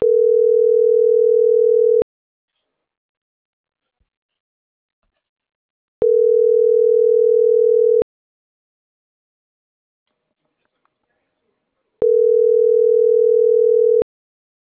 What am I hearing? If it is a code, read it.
Ringback tone (North America), 440 + 480 Hz.
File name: CENTRAL NAVIERAS / out onelinecapacitacion - .88605